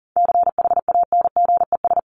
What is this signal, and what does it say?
Morse code, heard as a keyed tone, about 40 wpm, 700 Hz. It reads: CHANGES
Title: ES:S Suffix - Top 100 Words with Base Words 40wpm